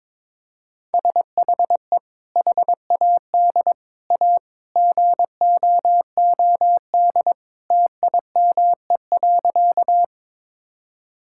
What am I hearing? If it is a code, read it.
Morse code, heard as a keyed tone, about 22 wpm, 700 Hz. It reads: SHE HAD A GOOD TIME.
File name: practice / 22wpm / Sentences with 2-Word Phrases - 22wpm